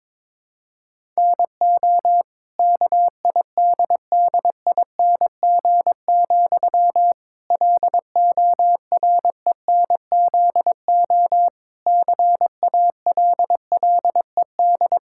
Morse code, heard as a keyed tone, about 22 wpm, 700 Hz. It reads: NO KIDDING, LORENZO CALLED
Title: Pangram Challenge v2 - 22wpm